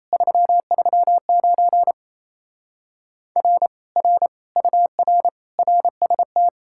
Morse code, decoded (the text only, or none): 339 R R UR RST